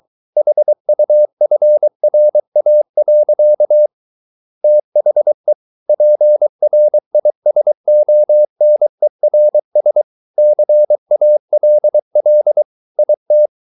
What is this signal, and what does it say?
Morse code, heard as a keyed tone, about 23 wpm, 600 Hz. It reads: HUFRA. THE PRISONERS CALL IT